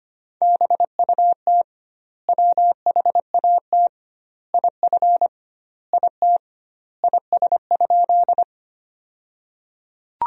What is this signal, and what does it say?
Morse code, heard as a keyed tone, about 25 wpm, 700 Hz. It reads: BUT WHAT IF IT IS?
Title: 2x Word-Spacing - Sentences from Top 100 Words - 25wpm